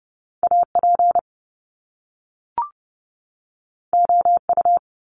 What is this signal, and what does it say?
Morse code, heard as a keyed tone, about 30 wpm, 700 Hz. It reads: AP OU
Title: Top Two Letter Combinations in Words 30wpm